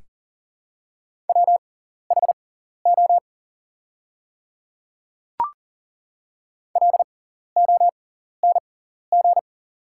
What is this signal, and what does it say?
Morse code, heard as a keyed tone, about 40 wpm, spaced as for 14 wpm, 700 Hz. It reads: WHO LONG
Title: 6x Character Spacing - Top 200 Words - 40wpm